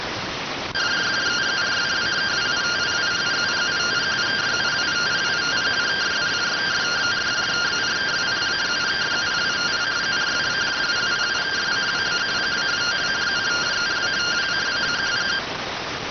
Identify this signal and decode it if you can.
Two-tone teleprinter signal, about 170 Hz shift.